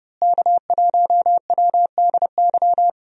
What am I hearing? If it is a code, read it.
Morse code, heard as a keyed tone, about 30 wpm, 700 Hz. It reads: K1WDY
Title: USA Call Signs - 5 Characters - Repeated 2x - 30wpm